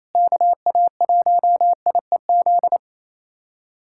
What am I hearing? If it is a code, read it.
Morse code, heard as a keyed tone, about 28 wpm, 690 Hz. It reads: KA1IEZ
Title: SOTA QSO Practice 28wpm